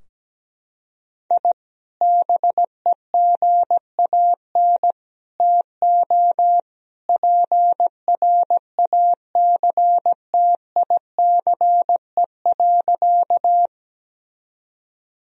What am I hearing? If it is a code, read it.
Morse code, heard as a keyed tone, about 17 wpm, 700 Hz. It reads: I BEGAN TO PRACTICE.
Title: Sentences from Top 1000 Words - Review Part 4 - 17wpm